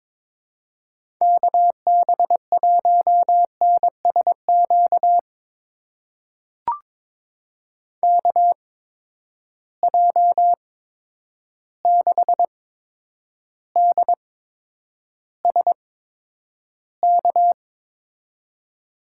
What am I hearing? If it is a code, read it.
Morse code, heard as a keyed tone, about 22 wpm, spaced as for 9 wpm, 700 Hz. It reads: KB1NSQ KJ6DSK
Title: USA Call Signs - 6 Characters - 8x Character Spacing - 22wpm